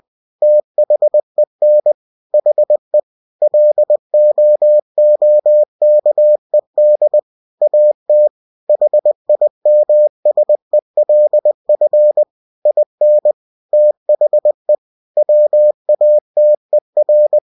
Morse code, heard as a keyed tone, about 20 wpm, 600 Hz. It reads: THEN HE LOOKED AT HIMSELF IN THE WATER
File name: cw-books / winnie-the-pooh / winnie-20wpm-page-0044